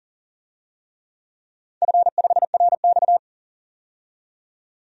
Morse code, heard as a keyed tone, about 40 wpm, 700 Hz. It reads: F5RX
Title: DXCC-100 Call Signs - 4 Characters - 40wpm